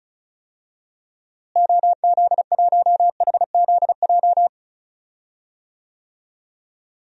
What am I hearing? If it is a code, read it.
Morse code, heard as a keyed tone, about 34 wpm, 700 Hz. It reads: OZ1HZJ
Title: DXCC-100 Call Signs - 6 Characters - Repeated 3x - 35wpm